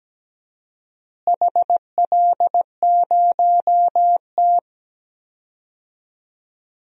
Morse code, heard as a keyed tone, about 17 wpm, 700 Hz. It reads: HL0T
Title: DXCC-100 Call Signs - 4 Characters - Repeated 3x - 17wpm